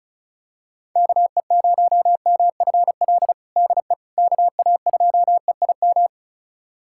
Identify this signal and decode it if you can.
Morse code, heard as a keyed tone, about 36 wpm, 700 Hz. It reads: KE0MFL DE KA2EIM
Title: USA Callsign DE Callsign - 6 Characters - 35wpm